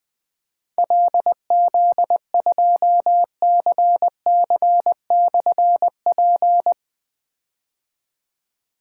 Morse code, heard as a keyed tone, about 20 wpm, 700 Hz. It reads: LZ2CC/P
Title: Difficult Call Signs 20wpm